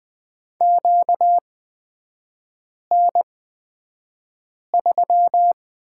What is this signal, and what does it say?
Morse code, heard as a keyed tone, about 20 wpm, spaced as for 5 wpm, 700 Hz. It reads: QN3
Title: 236 Lesson 38e - Letter q - Three Characters 20wpm